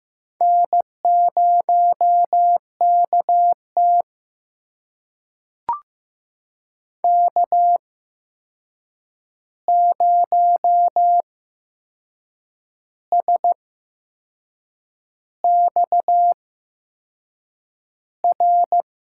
Morse code, heard as a keyed tone, about 15 wpm, spaced as for 6 wpm, 700 Hz. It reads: N0KT K0SXR
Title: USA Call Signs - 8x Character Spacing - 15wpm